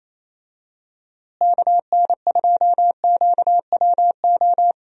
Morse code, heard as a keyed tone, about 28 wpm, 700 Hz. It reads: KN2QWO
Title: USA Call Signs - 6 Characters - Repeated 2x - 28wpm